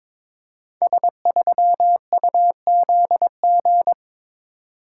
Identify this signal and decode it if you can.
Morse code, heard as a keyed tone, about 23 wpm, 690 Hz. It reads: S3UZG
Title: Code Groups - 5 Letter:Numbers - 22wpm